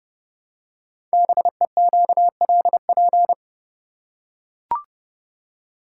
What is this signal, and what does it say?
Morse code, heard as a keyed tone, about 30 wpm, 700 Hz. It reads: BEQLP
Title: Code Groups - 5 Letters - 30wpm